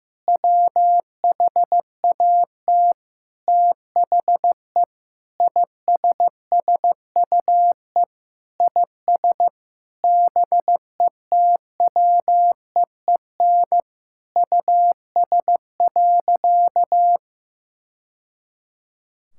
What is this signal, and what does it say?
Morse code, heard as a keyed tone, about 15 wpm, 700 Hz. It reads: WHAT THE ISSUE IS BETWEEN US.
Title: Sentences from Top 1000 Words - Review Part 6 - 15wpm